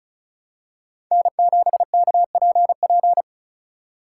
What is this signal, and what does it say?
Morse code, heard as a keyed tone, about 36 wpm, 700 Hz. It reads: N7KPP
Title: USA Call Signs - 5 Characters - 3x Character Spacing - 35wpm